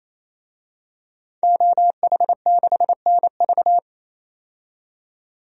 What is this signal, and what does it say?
Morse code, heard as a keyed tone, about 28 wpm, 700 Hz. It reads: OH6NV
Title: DXCC-100 Call Signs - 5 Characters - 28wpm